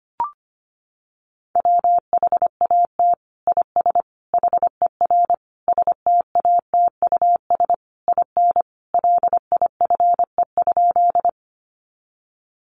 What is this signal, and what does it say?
Morse code, heard as a keyed tone, about 25 wpm, 700 Hz. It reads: WHAT IS HER STATUS IN LIFE?
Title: Sentences from Top 1400 Words 25wpm